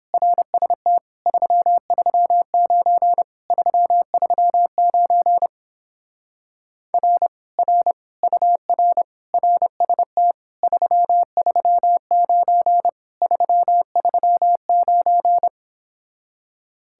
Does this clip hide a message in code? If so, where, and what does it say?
Yes: Morse code from 0.1 to 15.5 s, heard as a keyed tone, about 29 wpm, 700 Hz. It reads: RST 339 339 R R UR RST 339 339